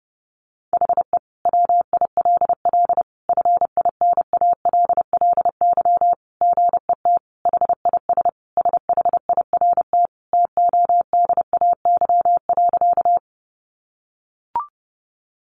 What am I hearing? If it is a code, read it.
Morse code, heard as a keyed tone, about 28 wpm, 700 Hz. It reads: HE WILL FINALLY GET HIS SHIRT TODAY.
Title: Additional Easy Phrases 30wpm